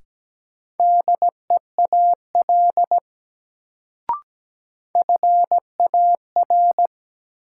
Morse code, heard as a keyed tone, about 17 wpm, 700 Hz. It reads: DEAL FAR
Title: Sets of 3 Words - New Word Drill - 17wpm